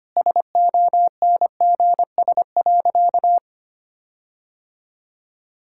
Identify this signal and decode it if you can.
Morse code, heard as a keyed tone, about 25 wpm, 700 Hz. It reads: SONGS.
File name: practice / 25wpm / Sentences with 4-Word Phrases - 25wpm